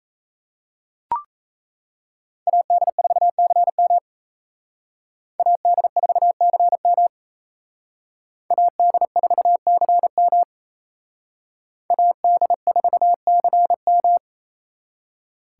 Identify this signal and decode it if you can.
Morse code, heard as a keyed tone, about 35 wpm, 700 Hz. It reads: AD4CM AD4CM AD4CM AD4CM